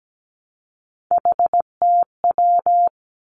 Morse code, heard as a keyed tone, about 17 wpm, 700 Hz. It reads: HTW
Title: Three Challenging Letter-Number Combinations 17wpm